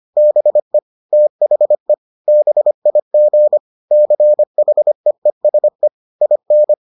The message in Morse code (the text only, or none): BE THE BIG CHEESE IN